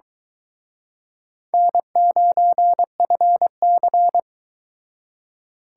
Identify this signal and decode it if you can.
Morse code, heard as a keyed tone, about 24 wpm, 700 Hz. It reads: N9FC